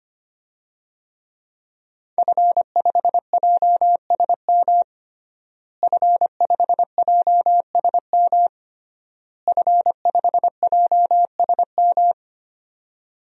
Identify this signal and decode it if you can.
Morse code, heard as a keyed tone, about 25 wpm, 700 Hz. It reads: F5JSM F5JSM F5JSM
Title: DXCC-100 Call Signs - 5 Characters - Repeated 3x - 25wpm